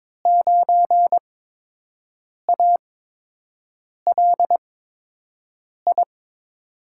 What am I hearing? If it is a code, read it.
Morse code, heard as a keyed tone, about 22 wpm, spaced as for 6 wpm, 700 Hz. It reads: 9ALI
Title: USA Call Signs - 5 Characters - 8x Character Spacing - 22wpm